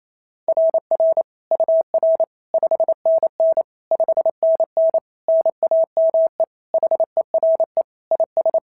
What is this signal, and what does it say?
Morse code, heard as a keyed tone, about 27 wpm, 690 Hz. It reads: RR UR 5NN 5NN NAME HERE IS
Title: SOTA QSO Practice 28wpm